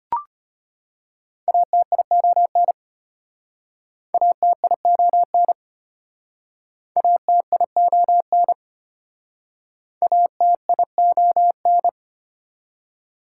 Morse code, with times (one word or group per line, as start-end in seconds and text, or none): 1.5-2.7 ATION
4.1-5.5 ATION
7.0-8.5 ATION
10.0-11.9 ATION